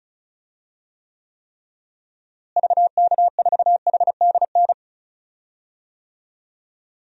Morse code, heard as a keyed tone, about 36 wpm, 700 Hz. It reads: VK4HDN
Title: DXCC-100 Call Signs - 6 Characters - Repeated 3x - 35wpm